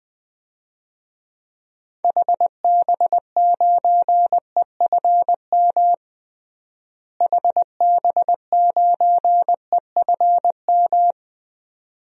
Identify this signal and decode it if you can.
Morse code, heard as a keyed tone, about 20 wpm, 700 Hz. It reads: HB9EFM HB9EFM
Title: DXCC-100 Call Signs - 6 Characters - Repeated 2x - 20wpm